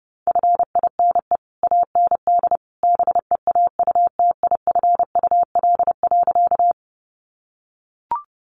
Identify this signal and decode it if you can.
Morse code, heard as a keyed tone, about 30 wpm, 700 Hz. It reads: FINE AND BEAUTIFUL.
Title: Sentences from Top 900 Words 30wpm